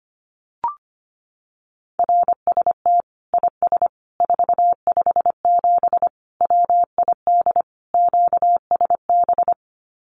Morse code, heard as a keyed tone, about 25 wpm, 700 Hz. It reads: RST IS 457 WID QSB
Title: Common QSO Phrases 25wpm